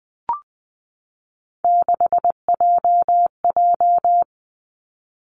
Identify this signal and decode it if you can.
Morse code, heard as a keyed tone, about 20 wpm, 700 Hz. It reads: 6JJ
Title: Three Challenging Letter-Number Combinations 20wpm